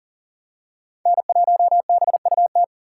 Morse code, heard as a keyed tone, about 40 wpm, 700 Hz. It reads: N1BUT
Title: USA Call Signs - 5 Characters - 3x Character Spacing - 40wpm